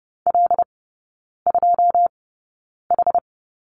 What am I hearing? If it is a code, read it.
Morse code, heard as a keyed tone, about 30 wpm, spaced as for 9 wpm, 700 Hz. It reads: L2H
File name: Lesson 18e - Letter m - Three Characters 30wpm